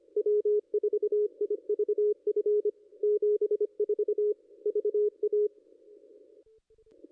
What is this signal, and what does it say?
Morse code, heard as a keyed tone, about 24 wpm, 440 Hz. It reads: W4IVF 74 VA
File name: contests / ss-cw-k5tr-op-n5rz -0200